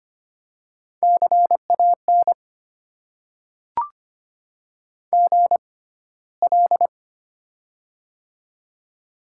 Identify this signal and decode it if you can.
Morse code, heard as a keyed tone, about 25 wpm, spaced as for 15 wpm, 700 Hz. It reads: CAN GL
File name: Top 90 QSO Elements - 6x Character Spacing - 25wpm